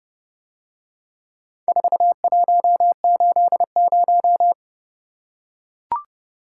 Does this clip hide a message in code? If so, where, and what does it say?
Yes: Morse code from 1.7 to 4.5 s, heard as a keyed tone, about 29 wpm, 700 Hz. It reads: 4180